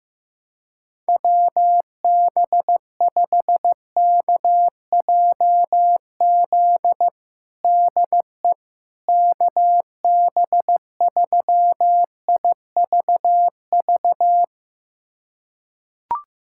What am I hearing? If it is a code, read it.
Morse code, heard as a keyed tone, about 16 wpm, 700 Hz. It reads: WB5KJZ DE KB3IVV
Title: USA Callsign DE Callsign - 6 Characters - 15wpm